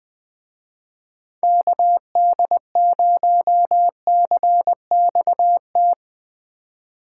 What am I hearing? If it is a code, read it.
Morse code, heard as a keyed tone, about 21 wpm, 700 Hz. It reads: KD0CXT